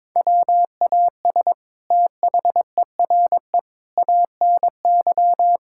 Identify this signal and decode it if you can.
Morse code, heard as a keyed tone, about 22 wpm, 700 Hz. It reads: WAS THERE ANY
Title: Sentences from Top 1000 Words - Review Part 6 - 22wpm